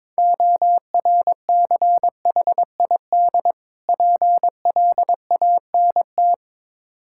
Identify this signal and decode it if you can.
Morse code, heard as a keyed tone, about 22 wpm, 690 Hz. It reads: ORCHID PLANT
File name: Difficult Words with Hints - 22wpm